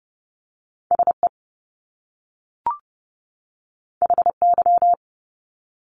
Morse code, heard as a keyed tone, about 30 wpm, 700 Hz. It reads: SE HY
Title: Top Two Letter Combinations in Words 30wpm